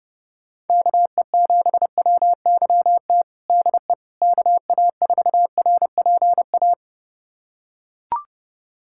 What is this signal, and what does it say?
Morse code, heard as a keyed tone, about 30 wpm, 700 Hz. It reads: KE7WYT DE KA4RPA
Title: USA Callsign DE Callsign 30wpm